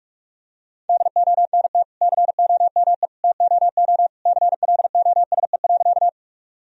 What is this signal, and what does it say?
Morse code, heard as a keyed tone, about 41 wpm, 700 Hz. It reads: DONT COME TOO CLOSE.
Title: Sentences from Top 1000 Words - Review Part 1 - 45wpm